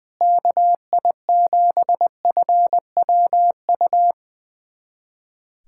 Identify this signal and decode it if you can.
Morse code, heard as a keyed tone, about 21 wpm, 700 Hz. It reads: KI7FWU
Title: USA Callsign DE Callsign - 6 Characters - 20wpm